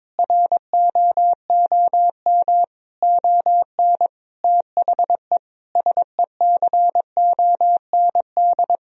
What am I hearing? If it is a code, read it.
Morse code, heard as a keyed tone, about 22 wpm, 700 Hz. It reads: ROOM ON THE SECOND
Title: Sentences from Top 1000 Words - Review Part 3 - 22wpm